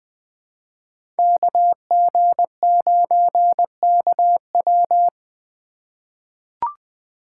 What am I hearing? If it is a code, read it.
Morse code, heard as a keyed tone, about 20 wpm, 700 Hz. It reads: KG9KW
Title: USA Call Signs - 5 Characters - 2x Character Spacing - 20wpm